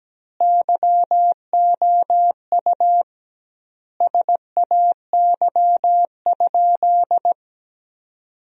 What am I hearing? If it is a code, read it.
Morse code, heard as a keyed tone, about 17 wpm, 700 Hz. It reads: YOU SAY?
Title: 2x Word-Spacing - Sentences from Top 400 Words - 17wpm